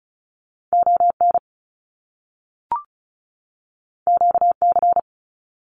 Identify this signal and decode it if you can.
Morse code, heard as a keyed tone, about 36 wpm, 700 Hz. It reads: ON QC
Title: Canadian Provinces and Territories 35wpm